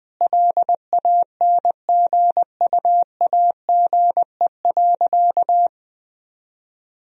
Morse code, heard as a keyed tone, about 21 wpm, 700 Hz. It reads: LANGUAGE.
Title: Sentences from Top 1000 Words - Review Part 3 - 20wpm